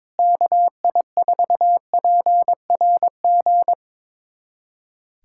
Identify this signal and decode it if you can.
Morse code, heard as a keyed tone, about 22 wpm, 700 Hz. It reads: KI4PRG
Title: USA Callsign DE Callsign - 6 Characters - 22wpm